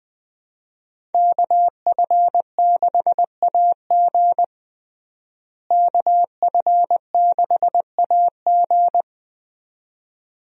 Morse code, heard as a keyed tone, about 21 wpm, 700 Hz. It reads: KF6AG KF6AG
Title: USA Call Signs - 5 Characters - Repeated 2x - 20wpm